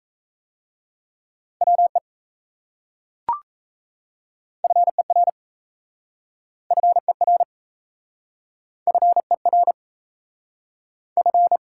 Morse code, heard as a keyed tone, about 37 wpm, 700 Hz. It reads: WE FER FER FER F